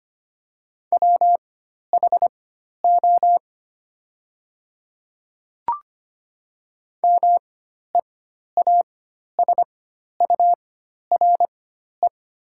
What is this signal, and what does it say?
Morse code, heard as a keyed tone, about 25 wpm, spaced as for 12 wpm, 700 Hz. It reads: WHO MEASURE
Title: 4x Character Spacing - Top 500 Words - 25wpm